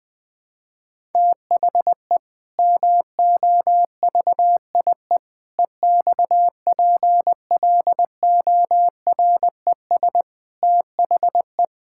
Morse code, heard as a keyed tone, about 20 wpm, 700 Hz. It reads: THE MOVIE EXPLORES THE
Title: Sentences with 6-Word Phrases - 20wpm